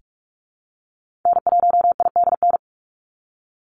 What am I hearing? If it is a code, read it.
Morse code, heard as a keyed tone, about 46 wpm, 700 Hz. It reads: N1IDN
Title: NA SOTA Activator Callsigns 45wpm